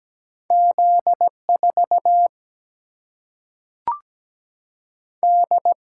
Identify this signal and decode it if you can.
Morse code, heard as a keyed tone, about 17 wpm, 700 Hz. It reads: Z4 D
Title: Single-Letter Single-Number 17wpm